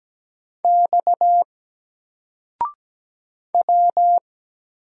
Morse code, heard as a keyed tone, about 17 wpm, 700 Hz. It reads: X W
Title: Single Character (Letter:Number:Punctuation:Prosign) Alt 17wpm